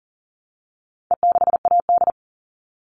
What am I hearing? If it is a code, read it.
Morse code, heard as a keyed tone, about 40 wpm, 700 Hz. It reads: E6AD
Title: All Cut Numbers - 4-Digit 40wpm